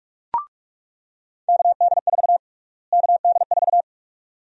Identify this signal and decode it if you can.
Morse code, heard as a keyed tone, about 41 wpm, 700 Hz. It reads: KD4 KD4
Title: USA Call Sign Prefixes - Repeated 2x - 45wpm